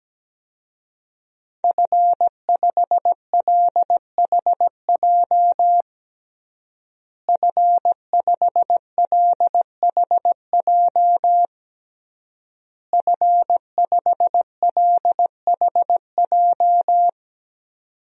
Morse code, heard as a keyed tone, about 17 wpm, 700 Hz. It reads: F5LHJ F5LHJ F5LHJ
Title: DXCC-100 Call Signs - 5 Characters - Repeated 3x - 17wpm